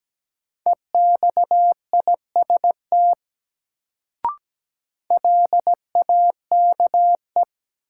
Morse code, heard as a keyed tone, about 17 wpm, 700 Hz. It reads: EXIST LAKE
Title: Sentences from Top 1100 Words - New Word Drill Alt - 17wpm